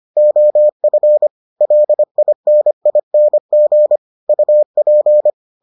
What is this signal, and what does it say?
Morse code, heard as a keyed tone, about 25 wpm, 600 Hz. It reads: OF LINING UP